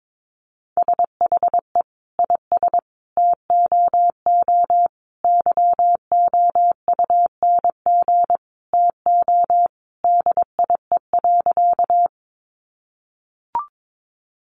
Morse code, heard as a keyed tone, about 22 wpm, 700 Hz. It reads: SHE IS TOO YOUNG TO DIE.
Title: Sentences from Top 1300 Words 22wpm